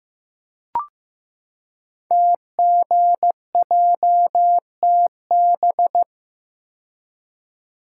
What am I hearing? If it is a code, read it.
Morse code, heard as a keyed tone, about 15 wpm, 700 Hz. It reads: TGJTB
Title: Code Groups - 5 Letters - 15wpm